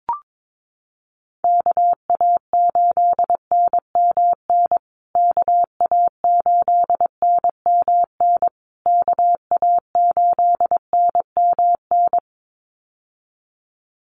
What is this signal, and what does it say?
Morse code, heard as a keyed tone, about 22 wpm, 700 Hz. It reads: KA8NMN KA8NMN KA8NMN
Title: USA Call Signs Repeated 3x 22wpm